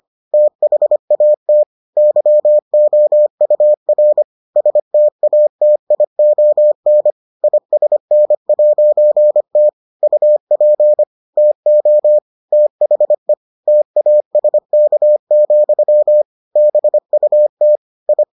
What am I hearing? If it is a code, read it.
Morse code, heard as a keyed tone, about 25 wpm, 600 Hz. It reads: THAT YOUR STATION ISN'T UP TO THE TASK, BUT I